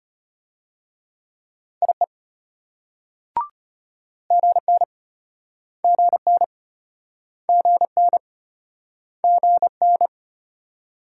Morse code, heard as a keyed tone, about 33 wpm, 700 Hz. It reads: IE GN GN GN GN